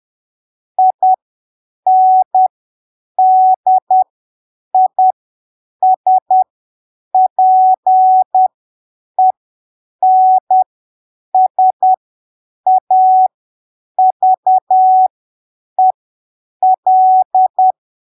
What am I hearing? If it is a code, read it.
Morse code, heard as a keyed tone, about 10 wpm, spaced as for 7 wpm, 750 Hz. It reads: INDISPENSAVEL
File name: arquivos / texto / texto 09 10ppm